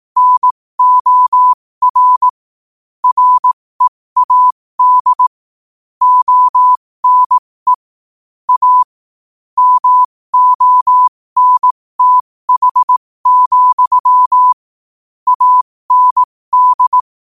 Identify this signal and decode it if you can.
Morse code, heard as a keyed tone, about 18 wpm, spaced as for 15 wpm, 1000 Hz. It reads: NOR READ ONE A MONTH, AND
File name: qotdfiles / 15wpm